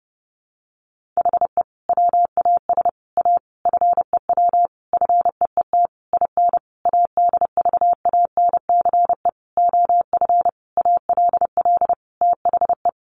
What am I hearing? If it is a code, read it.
Morse code, heard as a keyed tone, about 30 wpm, 700 Hz. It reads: HE WAS A FEW FEET IN ADVANCE OF ALL THE
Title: Sentences from Top 1400 Words 30wpm